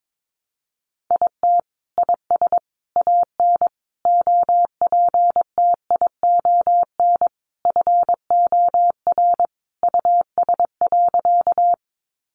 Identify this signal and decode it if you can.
Morse code, heard as a keyed tone, about 22 wpm, 700 Hz. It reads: IT IS AN OPTION FOR US.
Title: Sentences from Top 1300 Words 22wpm